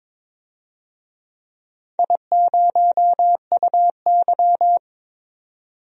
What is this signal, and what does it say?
Morse code, heard as a keyed tone, about 23 wpm, 700 Hz. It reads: I0UY
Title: DXCC-100 Call Signs - 4 Characters - 22wpm